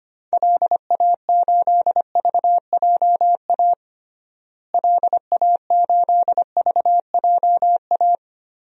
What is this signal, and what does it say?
Morse code, heard as a keyed tone, about 25 wpm, 700 Hz. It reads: LA8VJA LA8VJA
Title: DXCC-100 Call Signs - 6 Characters - Repeated 3x - 25wpm